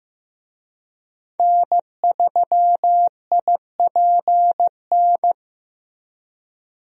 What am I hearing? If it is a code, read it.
Morse code, heard as a keyed tone, about 15 wpm, 700 Hz. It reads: N3IPN
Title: USA Call Signs - 5 Characters - Repeated 3x - 15wpm